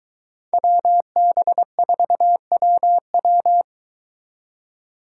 Morse code, heard as a keyed tone, about 23 wpm, 700 Hz. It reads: WB4WW